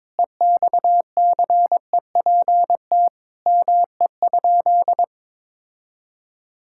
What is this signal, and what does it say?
Morse code, heard as a keyed tone, about 22 wpm, 700 Hz. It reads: EXCEPT ME?
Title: Sentences from Top 1600 Words 22wpm